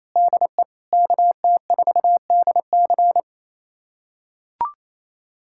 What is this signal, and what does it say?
Morse code, heard as a keyed tone, about 28 wpm, 690 Hz. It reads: DE KT4DC
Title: USA Callsign DE Callsign - 5 Characters - 28wpm